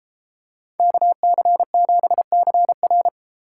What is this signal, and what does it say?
Morse code, heard as a keyed tone, about 33 wpm, 700 Hz. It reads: KC7CR